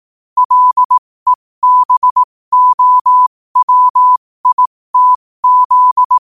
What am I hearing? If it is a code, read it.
Morse code, heard as a keyed tone, about 18 wpm, spaced as for 15 wpm, 1000 Hz. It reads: LEBOWITZ